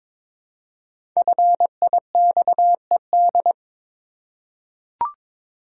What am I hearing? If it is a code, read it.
Morse code, heard as a keyed tone, about 22 wpm, 690 Hz. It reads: FIXED
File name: 5 Letter Words - First 3 Letters Given - 22wpm